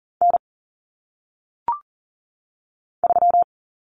Morse code, heard as a keyed tone, about 42 wpm, 700 Hz. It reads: N 3
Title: Common Cut Numbers - 1-Digit 40wpm